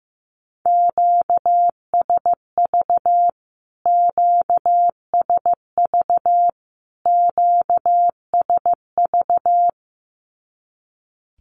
Morse code, heard as a keyed tone, about 15 wpm, 700 Hz. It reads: QSV QSV QSV
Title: Q-Codes Repeated 3x 15wpm